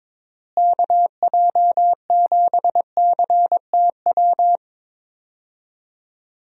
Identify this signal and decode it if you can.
Morse code, heard as a keyed tone, about 22 wpm, 700 Hz. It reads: KJ7CTW
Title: USA Call Signs - 6 Characters - Repeated 3x - 22wpm